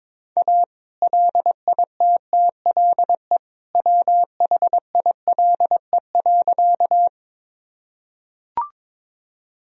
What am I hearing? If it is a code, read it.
Morse code, heard as a keyed tone, about 22 wpm, 700 Hz. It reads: A LITTLE WHILE.
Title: Sentences from Top 1000 Words - Review Part 1 - 22wpm